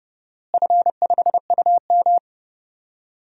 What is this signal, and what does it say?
Morse code, heard as a keyed tone, about 30 wpm, 700 Hz. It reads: F5UM
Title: DXCC-100 Call Signs - 4 Characters - Repeated 2x - 30wpm